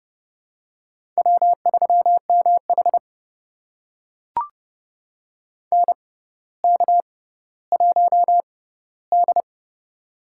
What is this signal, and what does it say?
Morse code, heard as a keyed tone, about 30 wpm, spaced as for 15 wpm, 700 Hz. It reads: W3MH NK1D
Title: USA Call Signs - 4 Characters - 6x Character Spacing - 30wpm